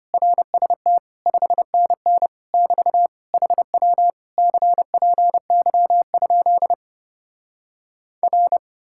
Morse code, heard as a keyed tone, about 33 wpm, 700 Hz. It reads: RST 5NN <BT> HW CPY? R